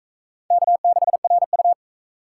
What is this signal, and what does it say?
Morse code, heard as a keyed tone, about 42 wpm, 700 Hz. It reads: K6RU